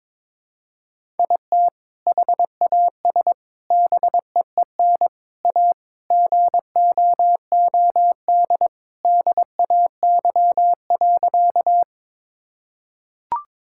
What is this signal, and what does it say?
Morse code, heard as a keyed tone, about 21 wpm, 700 Hz. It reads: IT HAS BEEN A GOOD DAY.
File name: practice / 22wpm / Sentences with 2-Word Phrases - 22wpm